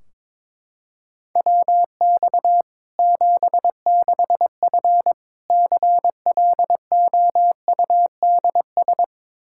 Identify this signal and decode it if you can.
Morse code, heard as a keyed tone, about 22 wpm, 700 Hz. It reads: WX 76F CLOUDS
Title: Practice QSO - Exchange 2 - Info - 22wpm